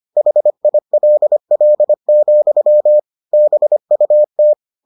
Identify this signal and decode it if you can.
Morse code, heard as a keyed tone, about 25 wpm, 600 Hz. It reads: HILL, BUT